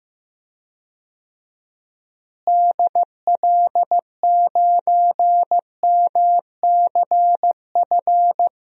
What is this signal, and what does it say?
Morse code, heard as a keyed tone, about 15 wpm, 700 Hz. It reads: DL9MCF
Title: DXCC-100 Call Signs - 6 Characters - Repeated 3x - 15wpm